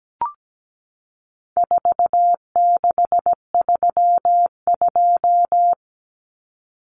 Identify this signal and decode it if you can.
Morse code, heard as a keyed tone, about 17 wpm, 700 Hz. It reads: 4632
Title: Common Cut Numbers - 4-Digit 17wpm